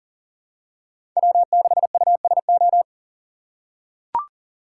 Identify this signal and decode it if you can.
Morse code, heard as a keyed tone, about 40 wpm, 700 Hz. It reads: W6USO
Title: USA Call Signs - 2x Character Spacing - 40wpm